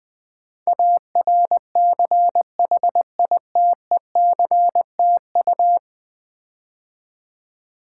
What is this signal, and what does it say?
Morse code, heard as a keyed tone, about 20 wpm, 700 Hz. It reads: ARCHITECTU
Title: Build A Word - 5 Letters At A Time - 12-Letter Words - 20wpm